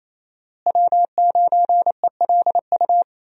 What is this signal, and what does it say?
Morse code, heard as a keyed tone, about 28 wpm, 700 Hz. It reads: W9ELU
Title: USA Call Signs - 5 Characters - Repeated 2x - 28wpm